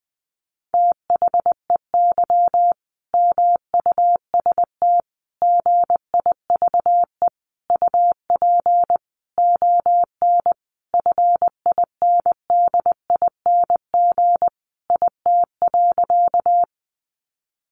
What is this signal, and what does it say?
Morse code, heard as a keyed tone, about 20 wpm, 700 Hz. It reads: THEY MUST GIVE UP ON FINDING IT.
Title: Sentences from Top 1100 Words 20wpm